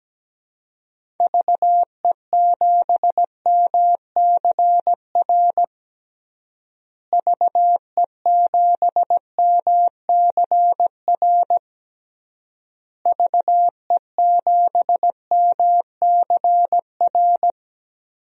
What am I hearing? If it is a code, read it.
Morse code, heard as a keyed tone, about 17 wpm, 700 Hz. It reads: VE7MCR VE7MCR VE7MCR
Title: DXCC-100 Call Signs - 6 Characters - Repeated 3x - 17wpm